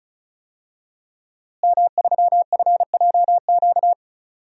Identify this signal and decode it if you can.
Morse code, heard as a keyed tone, about 34 wpm, 700 Hz. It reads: M3FJQ
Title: DXCC-100 Call Signs - 5 Characters - Repeated 2x - 35wpm